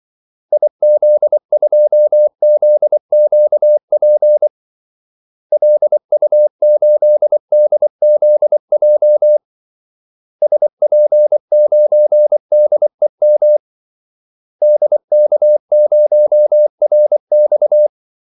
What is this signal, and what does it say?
Morse code, heard as a keyed tone, about 24 wpm, 600 Hz. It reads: IZ2ZQP LU8DZJ SP9DEM DK0RX
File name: Callsigns-24wpm